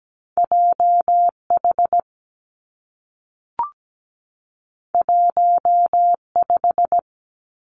Morse code, heard as a keyed tone, about 17 wpm, 700 Hz. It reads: J H 1 5
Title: Two Challenging Letter-Number Combinations 17wpm